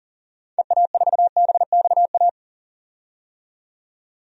Morse code, heard as a keyed tone, about 37 wpm, 700 Hz. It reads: EA4BXA
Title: DXCC-100 Call Signs - 6 Characters - 40wpm